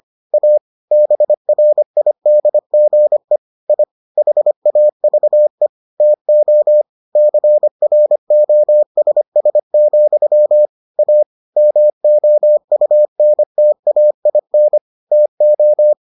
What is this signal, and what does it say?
Morse code, heard as a keyed tone, about 25 wpm, 600 Hz. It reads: A BRIDGE I HAVE TO CROSS, A MOUNTAIN TO